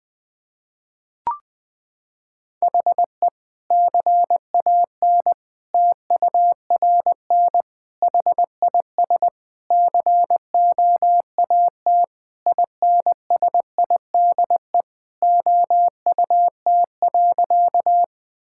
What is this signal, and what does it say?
Morse code, heard as a keyed tone, about 21 wpm, 700 Hz. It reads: HE CAN TURN HIS COAT INSIDE OUT.
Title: Sentences from Top 1600 Words 20wpm